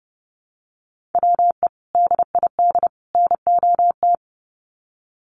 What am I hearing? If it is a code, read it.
Morse code, heard as a keyed tone, about 30 wpm, 700 Hz. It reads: WE DID NOT
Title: Sets of 3 Words 30wpm